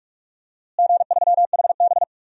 Morse code, heard as a keyed tone, about 46 wpm, 700 Hz. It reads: G3HB
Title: DXCC-100 Call Signs - 4 Characters - Repeated 2x - 45wpm